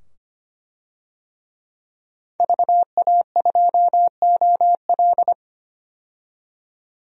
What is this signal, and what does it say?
Morse code, heard as a keyed tone, about 25 wpm, 700 Hz. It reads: VA2OL
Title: DXCC-100 Call Signs - 5 Characters - 25wpm